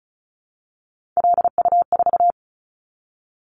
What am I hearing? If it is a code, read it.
Morse code, heard as a keyed tone, about 36 wpm, 700 Hz. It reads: LU4
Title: Callsign Prefixes for Top 50 DXCC - 35wpm